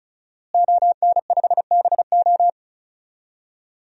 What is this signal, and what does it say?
Morse code, heard as a keyed tone, about 36 wpm, 700 Hz. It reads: ON5BO
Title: DXCC-100 Call Signs - 5 Characters - Repeated 2x - 35wpm